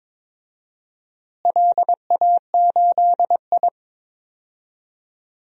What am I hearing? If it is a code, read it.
Morse code, heard as a keyed tone, about 21 wpm, 700 Hz. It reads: LA8I
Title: DXCC-100 Call Signs - 4 Characters - Repeated 2x - 22wpm